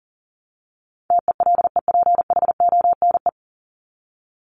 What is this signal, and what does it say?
Morse code, heard as a keyed tone, about 40 wpm, 700 Hz. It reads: TELEPHONE
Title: Top 1501-1750 Words 40wpm